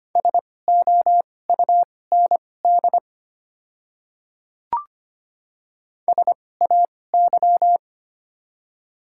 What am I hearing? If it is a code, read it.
Morse code, heard as a keyed tone, about 25 wpm, spaced as for 18 wpm, 690 Hz. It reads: SOUND SAY
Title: 2x Character Spacing - Top 200 Words - 25wpm